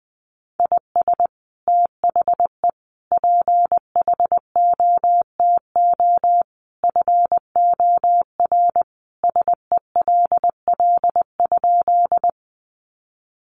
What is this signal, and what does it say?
Morse code, heard as a keyed tone, about 20 wpm, 700 Hz. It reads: IS THE PHOTO FOR SELL?
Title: Sentences from Top 1200 Words 20wpm